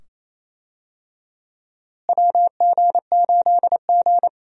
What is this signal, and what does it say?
Morse code, heard as a keyed tone, about 28 wpm, 700 Hz. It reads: WG8G
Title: USA Call Signs - 4 Characters - Repeated 2x - 28wpm